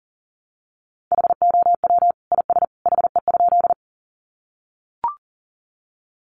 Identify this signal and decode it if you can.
Morse code, heard as a keyed tone, about 40 wpm, 700 Hz. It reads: HOW IS HE?
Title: Sentences from Top 100 Words 40wpm